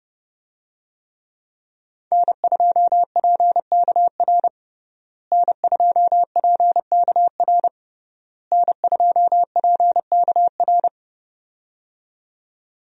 Morse code, heard as a keyed tone, about 30 wpm, 700 Hz. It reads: N2PKR N2PKR N2PKR
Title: USA Call Signs - 5 Characters - Repeated 3x - 30wpm